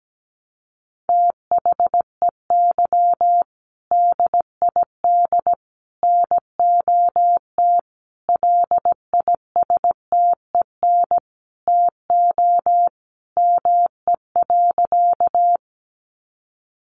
Morse code, heard as a keyed tone, about 17 wpm, 700 Hz. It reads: THEY DID NOT LISTEN TO ME.
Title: Sentences from Top 600 Words 17wpm